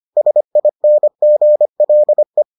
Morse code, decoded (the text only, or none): SINGLE